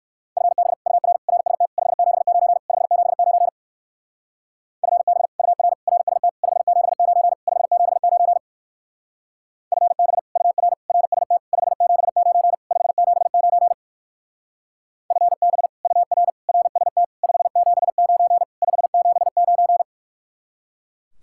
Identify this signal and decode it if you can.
Morse code, heard as a keyed tone, about 54 wpm, 700 Hz. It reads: FB UR RST 579 579 FB UR RST 579 579 FB UR RST 579 579 FB UR RST 579 579
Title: Speed-Racing Common QSO Phrases Encore - 45wpm